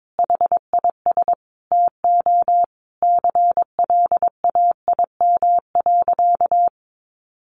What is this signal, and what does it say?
Morse code, heard as a keyed tone, about 22 wpm, 700 Hz. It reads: HIS TO CLAIM.
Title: Sentences from Top 1500 Words 22wpm